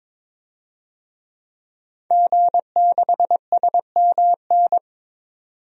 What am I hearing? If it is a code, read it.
Morse code, heard as a keyed tone, about 21 wpm, 700 Hz. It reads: G6SMN
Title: DXCC-100 Call Signs - 5 Characters - 22wpm